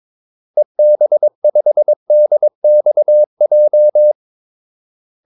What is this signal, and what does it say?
Morse code, heard as a keyed tone, about 22 wpm, 600 Hz. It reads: EB5DXJ
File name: Callsigns-22wpm